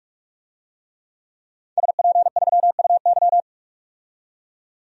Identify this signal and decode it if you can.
Morse code, heard as a keyed tone, about 42 wpm, 700 Hz. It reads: SP3UY
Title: DXCC-100 Call Signs - 5 Characters - 45wpm